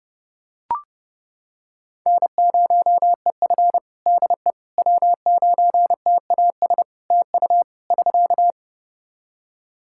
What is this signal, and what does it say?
Morse code, heard as a keyed tone, about 30 wpm, 700 Hz. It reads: N0EF DE W9TAS TU <SK>
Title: Practice QSO - Exchange 4 - Ending - 30wpm